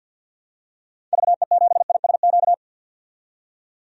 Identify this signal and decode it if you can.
Morse code, heard as a keyed tone, about 50 wpm, 700 Hz. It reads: VE7ISQ
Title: DXCC-100 Call Signs - 6 Characters - 50wpm